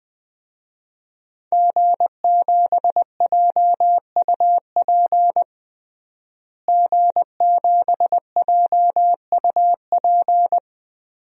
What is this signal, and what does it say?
Morse code, heard as a keyed tone, about 21 wpm, 700 Hz. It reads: G7JUP G7JUP
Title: DXCC-100 Call Signs - 5 Characters - Repeated 2x - 20wpm